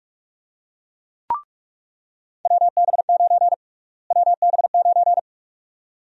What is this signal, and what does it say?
Morse code, heard as a keyed tone, about 47 wpm, 700 Hz. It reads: WB9 WB9
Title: USA Call Sign Prefixes - Repeated 2x - 45wpm